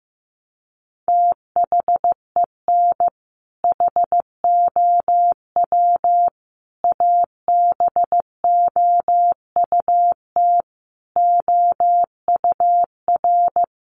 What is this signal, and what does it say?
Morse code, heard as a keyed tone, about 15 wpm, 700 Hz. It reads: THEN HOW ABOUT OUR
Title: Sentences from Top 900 Words 15wpm